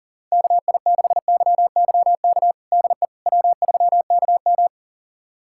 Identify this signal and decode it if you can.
Morse code, heard as a keyed tone, about 40 wpm, 700 Hz. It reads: KI6YYK DE W3KM
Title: USA Callsign DE Callsign 40wpm